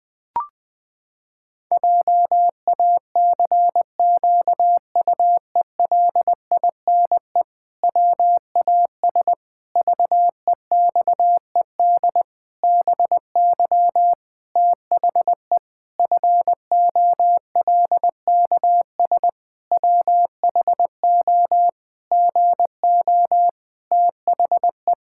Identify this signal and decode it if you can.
Morse code, heard as a keyed tone, about 20 wpm, 700 Hz. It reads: JACQUELINE WAS VEXED BY THE FOLKS WHO GO THE
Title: Pangram Challenge v2 - 20wpm